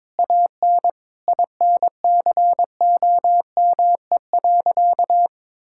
Morse code, heard as a keyed tone, about 22 wpm, 700 Hz. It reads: AN INCOME.
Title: Sentences from Top 1000 Words - Review Part 5 - 22wpm